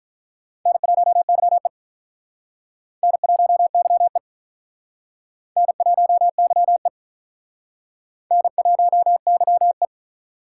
Morse code, heard as a keyed tone, about 46 wpm, 700 Hz. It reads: N1YE N1YE N1YE N1YE